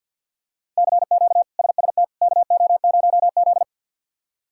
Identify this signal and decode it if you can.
Morse code, heard as a keyed tone, about 51 wpm, 700 Hz. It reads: CQ SST KO0Z
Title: K1USN SST QSO Practice - 50wpm